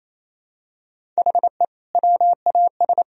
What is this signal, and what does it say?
Morse code, heard as a keyed tone, about 28 wpm, 700 Hz. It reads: HE WAS
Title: Sentences from Top 1000 Words - Review Part 1 - 28wpm